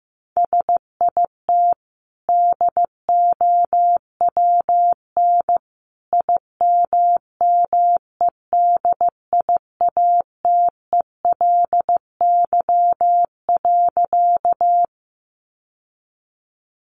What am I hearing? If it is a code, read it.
Morse code, heard as a keyed tone, about 15 wpm, 700 Hz. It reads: SIT DOWN IMMEDIATELY.
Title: Sentences from Top 1100 Words 15wpm